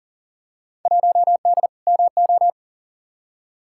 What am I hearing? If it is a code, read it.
Morse code, heard as a keyed tone, about 37 wpm, 700 Hz. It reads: 1D MO
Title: Field Day Exchanges - 40wpm